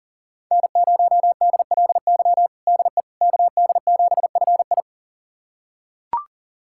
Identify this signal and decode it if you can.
Morse code, heard as a keyed tone, about 40 wpm, 700 Hz. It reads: N0DLY DE KD7FI
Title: USA Callsign DE Callsign - 5 Characters - 40wpm